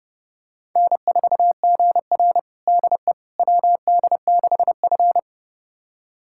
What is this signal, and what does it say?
Morse code, heard as a keyed tone, about 31 wpm, 700 Hz. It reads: N4GR DE WD6F
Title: USA Callsign DE Callsign - 4 Characters - 30wpm